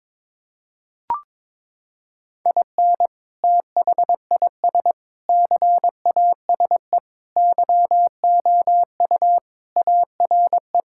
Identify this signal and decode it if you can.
Morse code, heard as a keyed tone, about 22 wpm, 700 Hz. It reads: IN THIS CASE YOU ARE
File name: practice / 22wpm / Sentences with 3-Word Phrases - 22wpm